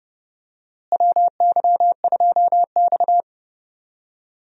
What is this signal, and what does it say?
Morse code, heard as a keyed tone, about 30 wpm, 700 Hz. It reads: WY2X
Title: USA Call Signs - 4 Characters - Repeated 2x - 30wpm